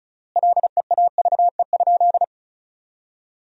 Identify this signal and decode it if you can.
Morse code, heard as a keyed tone, about 34 wpm, 700 Hz. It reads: LEAVE?
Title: 3x Word-Spacing - Sentences from Top 400 Words - 35wpm